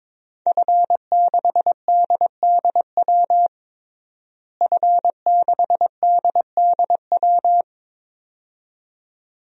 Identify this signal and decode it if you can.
Morse code, heard as a keyed tone, about 21 wpm, 700 Hz. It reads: F6DDW F6DDW
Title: DXCC-100 Call Signs - 5 Characters - Repeated 2x - 22wpm